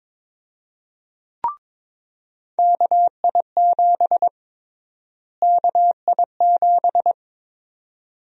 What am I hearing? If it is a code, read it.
Morse code, heard as a keyed tone, about 21 wpm, 700 Hz. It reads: KI7 KI7
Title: USA Call Sign Prefixes - Repeated 2x - 22wpm